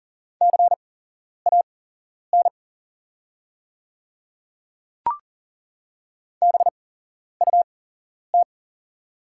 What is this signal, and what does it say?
Morse code, heard as a keyed tone, about 40 wpm, spaced as for 11 wpm, 690 Hz. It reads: CAN BUT
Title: 8x Character Spacing - Top 200 Words - 40wpm